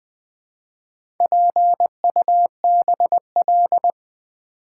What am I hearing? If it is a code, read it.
Morse code, heard as a keyed tone, about 20 wpm, 700 Hz. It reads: PUBL
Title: Build A Word - 4 Letters At A Time - 10-Letter Words - 20wpm